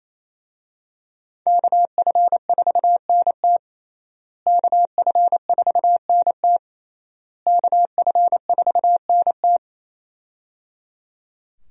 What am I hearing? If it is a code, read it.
Morse code, heard as a keyed tone, about 28 wpm, 700 Hz. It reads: KF4NT KF4NT KF4NT
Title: USA Call Signs - 5 Characters - Repeated 3x - 28wpm